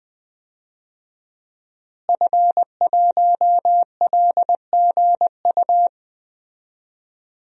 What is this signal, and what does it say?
Morse code, heard as a keyed tone, about 20 wpm, 700 Hz. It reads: F1LGU
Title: DXCC-100 Call Signs - 5 Characters - Repeated 2x - 20wpm